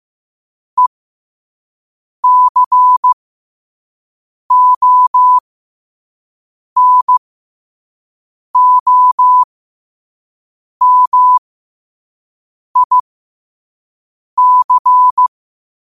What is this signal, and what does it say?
Morse code, heard as a keyed tone, about 15 wpm, spaced as for 5 wpm, 1000 Hz. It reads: ECONOMIC
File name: qotdfiles / 5wpm